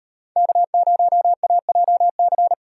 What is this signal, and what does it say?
Morse code, heard as a keyed tone, about 38 wpm, 700 Hz. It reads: K0AJC